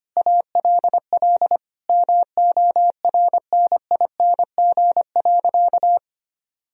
Morse code, heard as a keyed tone, about 25 wpm, 700 Hz. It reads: ALL MORNING.
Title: Sentences from Top 1000 Words - Review Part 3 - 25wpm